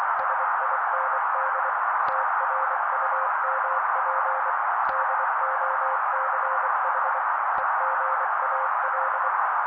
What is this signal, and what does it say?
Morse code, heard as a keyed tone, about 24 wpm, 530 Hz. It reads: SIND TRUMP DOCH EGAL
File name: daily / 23wpm noise